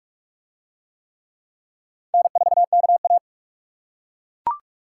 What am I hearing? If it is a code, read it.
Morse code, heard as a keyed tone, about 42 wpm, 700 Hz. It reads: N4KA